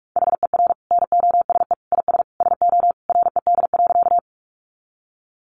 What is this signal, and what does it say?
Morse code, heard as a keyed tone, about 44 wpm, 700 Hz. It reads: HER NOSE IS SO RED.
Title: Easy Phrases 45wpm